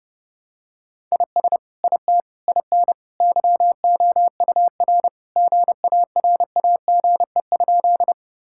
Morse code, heard as a keyed tone, about 30 wpm, 700 Hz. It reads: IS IT IN YOUR GARAGE?
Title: Sentences from Top 1600 Words 30wpm